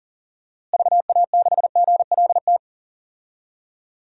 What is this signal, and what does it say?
Morse code, heard as a keyed tone, about 39 wpm, 700 Hz. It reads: VA6GLT
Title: DXCC-100 Call Signs - 6 Characters - Repeated 2x - 40wpm